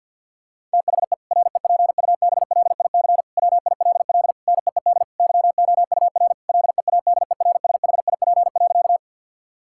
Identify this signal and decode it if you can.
Morse code, heard as a keyed tone, about 50 wpm, 700 Hz. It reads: THE REPUBLIC WILL NEED YOUR LEADERSHIP.
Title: Sentences from Top 1600 Words 50wpm